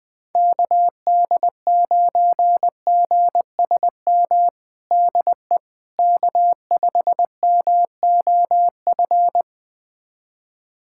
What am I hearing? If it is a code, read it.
Morse code, heard as a keyed tone, about 20 wpm, 700 Hz. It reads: KD9GSM DE K5MOF
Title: USA Callsign DE Callsign 20wpm